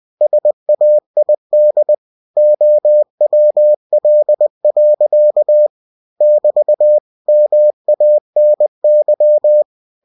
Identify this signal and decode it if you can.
Morse code, heard as a keyed tone, about 20 wpm, 600 Hz. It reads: SAID OWL. <BT> MANY
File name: cw-books / winnie-the-pooh / winnie-20wpm-page-0050